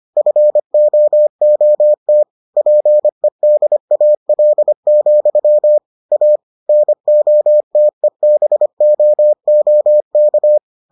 Morse code, heard as a keyed tone, about 25 wpm, 600 Hz. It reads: FOOT PEDAL, A NOTEBOOK